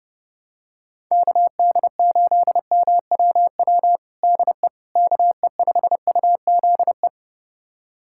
Morse code, heard as a keyed tone, about 30 wpm, 700 Hz. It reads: KD8MWW DE KE5UZE
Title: USA Callsign DE Callsign - 6 Characters - 30wpm